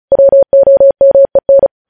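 Morse code, heard as a keyed tone, about 36 wpm, 550 Hz. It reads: WOMEN
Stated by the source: Morse code news headlines practice sent at 35 wpm.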